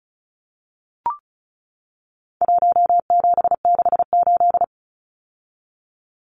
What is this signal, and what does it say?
Morse code, heard as a keyed tone, about 34 wpm, 700 Hz. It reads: 1768
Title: Numbers - Contest Style - 4-Digit 35wpm